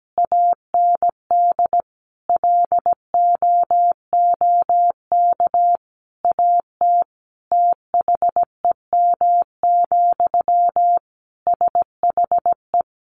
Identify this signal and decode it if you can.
Morse code, heard as a keyed tone, about 17 wpm, 700 Hz. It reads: AND LOOK AT THEM, SHE
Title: Sentences from Top 1400 Words 17wpm